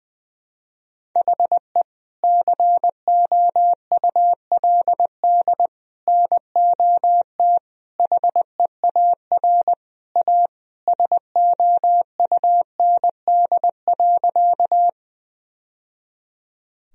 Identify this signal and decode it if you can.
Morse code, heard as a keyed tone, about 20 wpm, 700 Hz. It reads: HE COULD NOT HEAR A SOUND.
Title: Sentences from Top 1000 Words - Review Part 1 - 20wpm